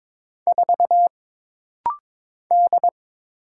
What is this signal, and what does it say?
Morse code, heard as a keyed tone, about 22 wpm, 700 Hz. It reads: 4 D
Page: This rapid-fire timing is geared to push your ICR (Instant Character Recognition) skills.